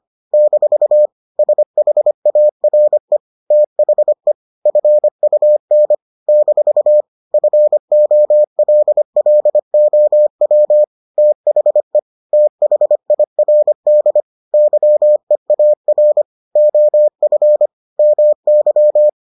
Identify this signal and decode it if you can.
Morse code, heard as a keyed tone, about 25 wpm, 600 Hz. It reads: - SHARE THE FUN - FOLLOW THE THIRD YEAR OF MY